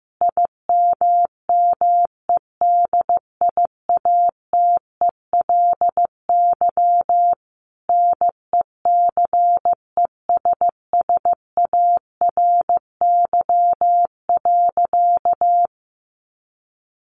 Morse code, heard as a keyed tone, about 15 wpm, 700 Hz. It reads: IMMEDIATELY NECESSARY.
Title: Sentences from Top 1100 Words 15wpm